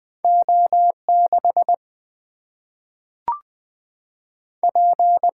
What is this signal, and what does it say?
Morse code, heard as a keyed tone, about 20 wpm, 700 Hz. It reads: O6 P
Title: Single-Letter Single-Number 20wpm